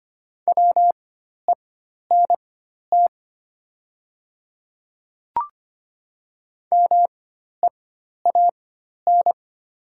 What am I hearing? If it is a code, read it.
Morse code, heard as a keyed tone, about 25 wpm, spaced as for 12 wpm, 690 Hz. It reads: WENT MEAN
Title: 4x Character Spacing - Top 200 Words - 25wpm